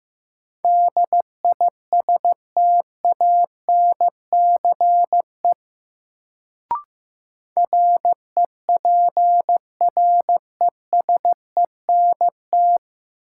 Morse code, heard as a keyed tone, about 15 wpm, 700 Hz. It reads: DISTANCE REPRESENT
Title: Sentences from Top 1200 Words - New Word Drill Alt - 15wpm